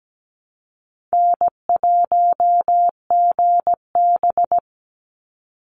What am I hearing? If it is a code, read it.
Morse code, heard as a keyed tone, about 17 wpm, 700 Hz. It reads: N1GB
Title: NA SOTA Chaser Callsigns 17wpm